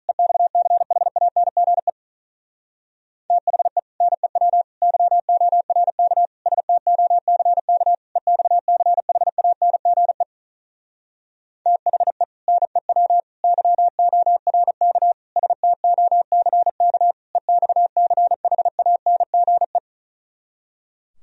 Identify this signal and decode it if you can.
Morse code, heard as a keyed tone, about 42 wpm, 700 Hz. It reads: EXCHANGE THE NEW YORK STOCK EXCHANGE THE NEW YORK STOCK EXCHANGE
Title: Speed-Racing - Sets of 5 Words 35wpm